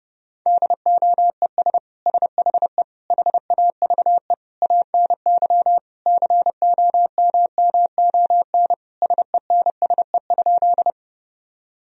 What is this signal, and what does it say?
Morse code, heard as a keyed tone, about 30 wpm, 700 Hz. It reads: DOES SHE HAVE ANY COMMON SENSE?
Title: Sentences from Top 1000 Words - Review Part 4 - 30wpm